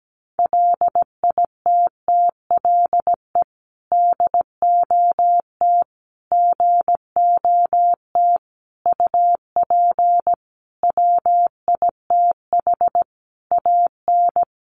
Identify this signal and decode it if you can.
Morse code, heard as a keyed tone, about 17 wpm, 700 Hz. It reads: LITTLE DOT GOT UP WITH AN
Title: Sentences from Top 1500 Words 17wpm